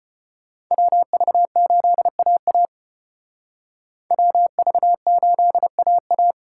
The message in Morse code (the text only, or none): WV8AA WV8AA